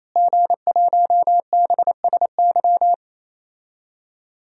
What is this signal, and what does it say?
Morse code, heard as a keyed tone, about 28 wpm, 700 Hz. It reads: G1BSY
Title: DXCC-100 Call Signs - 5 Characters - Repeated 3x - 28wpm